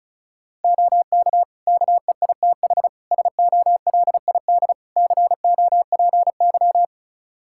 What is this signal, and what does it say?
Morse code, heard as a keyed tone, about 36 wpm, 700 Hz. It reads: OK KEITH SOLID COPY
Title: Practice QSO - Exchange 3 - More Info - 35wpm